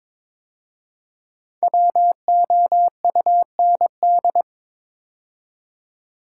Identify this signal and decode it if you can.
Morse code, heard as a keyed tone, about 22 wpm, 700 Hz. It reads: WOUND
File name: 5 Letter Words - First 3 Letters Given - 22wpm